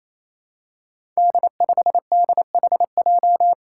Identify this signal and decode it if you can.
Morse code, heard as a keyed tone, about 27 wpm, 700 Hz. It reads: D5DHJ
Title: Five Challenging Letter-Number Combinations 28wpm